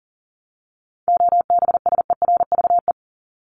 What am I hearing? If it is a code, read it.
Morse code, heard as a keyed tone, about 40 wpm, 700 Hz. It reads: OBSERVE
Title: ED Suffix - Top 100 Words with Base Words 40wpm